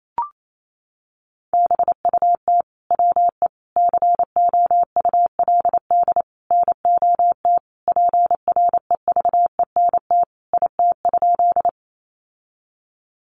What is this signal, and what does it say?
Morse code, heard as a keyed tone, about 28 wpm, 700 Hz. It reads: BUT WE COULD NOT PREVENT IT?
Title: Sentences from Top 1100 Words 28wpm